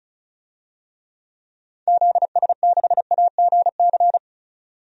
Morse code, heard as a keyed tone, about 34 wpm, 700 Hz. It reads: ZS6AGC
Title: DXCC-100 Call Signs - 6 Characters - 35wpm